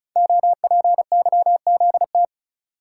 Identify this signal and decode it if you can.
Morse code, heard as a keyed tone, about 36 wpm, 700 Hz. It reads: OPYZT
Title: Code Groups - 5 Letters - 35wpm